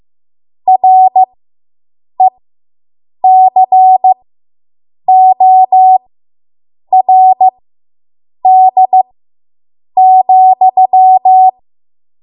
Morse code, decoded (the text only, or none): RECORD,